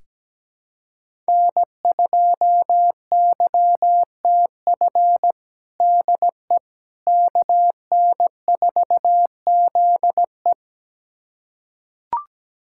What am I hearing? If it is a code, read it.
Morse code, heard as a keyed tone, about 17 wpm, 700 Hz. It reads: N2YTF DE KN4ZE
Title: USA Callsign DE Callsign - 5 Characters - 17wpm